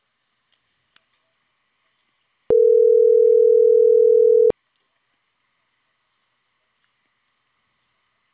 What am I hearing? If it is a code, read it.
Ringback tone (North America), 440 + 480 Hz.